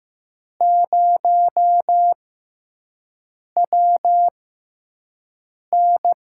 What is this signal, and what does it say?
Morse code, heard as a keyed tone, about 15 wpm, spaced as for 5 wpm, 700 Hz. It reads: 0WN
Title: USA Call Signs - 6x Character Spacing - 15wpm